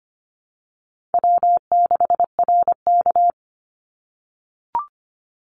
Morse code, heard as a keyed tone, about 25 wpm, 700 Hz. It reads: W6RK
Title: Easy Call Signs - 25wpm